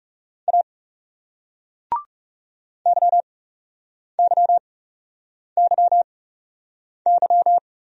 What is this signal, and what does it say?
Morse code, heard as a keyed tone, about 37 wpm, spaced as for 8 wpm, 700 Hz. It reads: A YYYY
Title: Speed-Racing Single Letters 30wpm